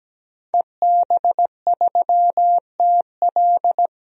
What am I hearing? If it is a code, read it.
Morse code, heard as a keyed tone, about 17 wpm, 700 Hz. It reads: EB3TL
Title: DXCC-100 Call Signs - 5 Characters - 17wpm